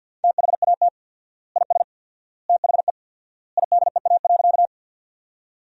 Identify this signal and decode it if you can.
Morse code, heard as a keyed tone, about 53 wpm, 700 Hz. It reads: THAT IS THE IDEA.
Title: 4x Word-Spacing - Sentences from Top 300 Words - 50wpm